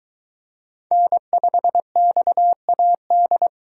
Morse code, heard as a keyed tone, about 22 wpm, 700 Hz. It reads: N5XAD